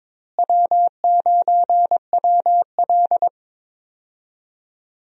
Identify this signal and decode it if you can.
Morse code, heard as a keyed tone, about 22 wpm, 700 Hz. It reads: W9WL
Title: USA Call Signs - 4 Characters - Repeated 3x - 22wpm